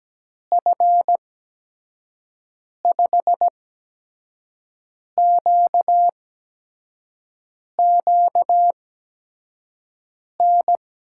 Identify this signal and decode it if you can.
Morse code, heard as a keyed tone, about 17 wpm, spaced as for 5 wpm, 700 Hz. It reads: F5QQN
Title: USA Call Signs - 6 Characters - 8x Character Spacing - 17wpm